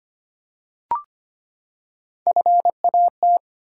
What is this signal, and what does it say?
Morse code, heard as a keyed tone, about 25 wpm, 700 Hz. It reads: FAT
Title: Three Letter Words - 25wpm